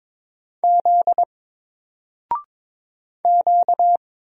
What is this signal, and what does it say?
Morse code, heard as a keyed tone, about 22 wpm, 700 Hz. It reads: Z Q
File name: Single-Letter-Number-22wpm